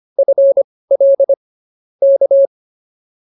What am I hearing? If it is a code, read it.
Morse code, heard as a keyed tone, about 24 wpm, spaced as for 18 wpm, 550 Hz. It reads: FL K
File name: POTA203 18